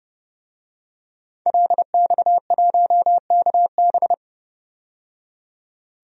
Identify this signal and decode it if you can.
Morse code, heard as a keyed tone, about 30 wpm, 700 Hz. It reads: LX1KB
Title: DXCC-100 Call Signs - 5 Characters - 30wpm